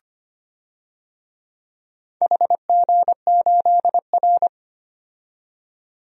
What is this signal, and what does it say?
Morse code, heard as a keyed tone, about 25 wpm, 700 Hz. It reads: HG8R
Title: DXCC-100 Call Signs - 4 Characters - 25wpm